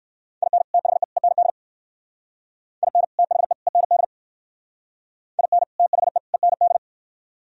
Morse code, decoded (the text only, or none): IN THE END IN THE END IN THE END